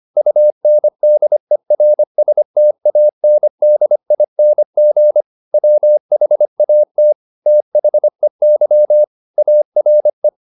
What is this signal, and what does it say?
Morse code, heard as a keyed tone, about 25 wpm, 600 Hz. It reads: UNDERSTANDING WHAT THEY ARE